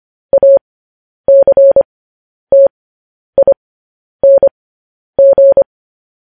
Morse code, heard as a keyed tone, about 25 wpm, spaced as for 10 wpm, 550 Hz. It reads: ACTING
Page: News Headlines in Morse Code at 10 WPM